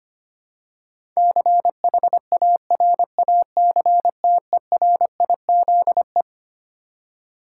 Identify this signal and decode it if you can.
Morse code, heard as a keyed tone, about 25 wpm, 700 Hz. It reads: CHARACTERIZE
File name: Twelve Letter Words - 25wpm